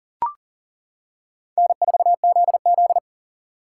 Morse code, heard as a keyed tone, about 37 wpm, 700 Hz. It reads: N4ZZ
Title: K1USN SST QSO Practice - 40wpm